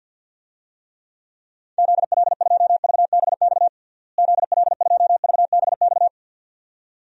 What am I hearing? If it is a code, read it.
Morse code, heard as a keyed tone, about 52 wpm, 700 Hz. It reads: ZL2VBX ZL2VBX
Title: DXCC-100 Call Signs - 6 Characters - Repeated 2x - 50wpm